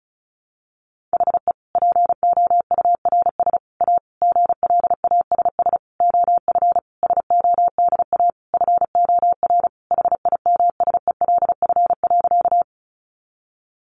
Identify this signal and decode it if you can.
Morse code, heard as a keyed tone, about 34 wpm, 700 Hz. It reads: HE POURS A GLASS OF SODA FOR HIMSELF.
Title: Additional Easy Phrases 35wpm